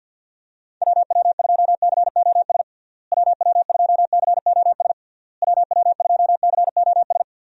Morse code, heard as a keyed tone, about 53 wpm, 700 Hz. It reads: WW2COS WW2COS WW2COS
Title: USA Call Signs - 6 Characters - Repeated 3x - 50wpm